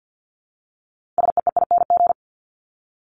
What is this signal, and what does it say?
Morse code, heard as a keyed tone, about 50 wpm, 700 Hz. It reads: SEEING
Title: ING Suffix - Top 100 Words with Base Words 50wpm